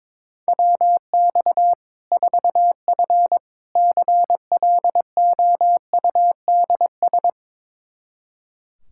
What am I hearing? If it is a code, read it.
Morse code, heard as a keyed tone, about 22 wpm, 700 Hz. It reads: WX 4F CLOUDS
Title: Practice QSO - Exchange 2 - Info - 22wpm